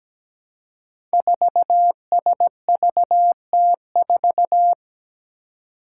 Morse code, heard as a keyed tone, about 17 wpm, 700 Hz. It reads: 4SVT4
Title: Five Challenging Letter-Number Combinations 17wpm